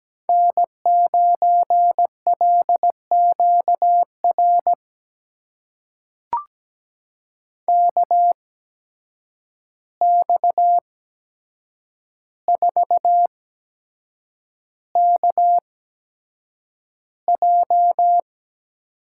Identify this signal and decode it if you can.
Morse code, heard as a keyed tone, about 17 wpm, spaced as for 7 wpm, 700 Hz. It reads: N9LQR KX4KJ
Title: USA Call Signs - 5 Characters - 8x Character Spacing - 17wpm